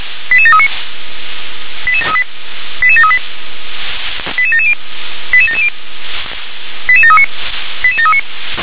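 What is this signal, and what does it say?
Five-tone selective call (ZVEI), heard as quick runs of five tones, tones of about 70 ms. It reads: 88739 88738 88738 88770 88770 88739 88739
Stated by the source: ZVEI five-tone selective call 88738, received by SDR.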